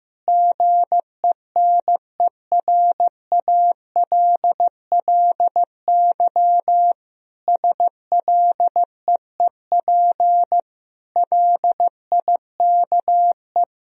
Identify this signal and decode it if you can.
Morse code, heard as a keyed tone, about 16 wpm, 700 Hz. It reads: GENERALLY SLEEP LIKE
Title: Sentences from Top 1000 Words - Review Part 4 - 15wpm